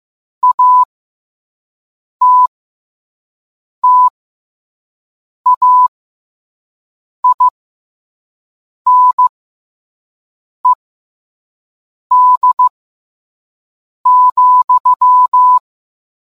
Morse code, heard as a keyed tone, about 15 wpm, spaced as for 5 wpm, 1000 Hz. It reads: ATTAINED,